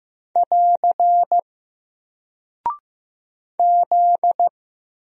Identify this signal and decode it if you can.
Morse code, heard as a keyed tone, about 15 wpm, 700 Hz. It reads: <AR> Z
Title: Single Character (Letter:Number:Punctuation:Prosign) Alt 15wpm